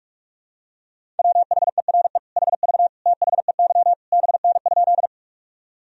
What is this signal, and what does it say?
Morse code, heard as a keyed tone, about 46 wpm, 700 Hz. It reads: WHERE HV THEY BN?
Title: Sentences from Top 200 Words and Abbreviations 45wpm